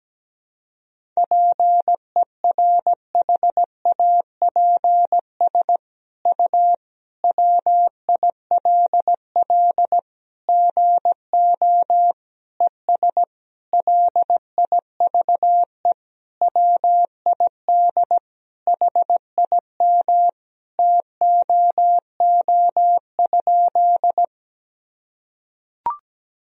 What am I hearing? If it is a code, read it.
Morse code, heard as a keyed tone, about 17 wpm, 700 Hz. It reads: PERHAPS U WILL GO ES LIVE WID HIM TOO?
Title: Sentences from Top 500 Words and Abbreviations 17wpm